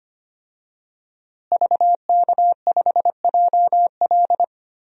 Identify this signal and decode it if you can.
Morse code, heard as a keyed tone, about 25 wpm, 700 Hz. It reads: VK5JL
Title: DXCC-100 Call Signs - 5 Characters - Repeated 3x - 25wpm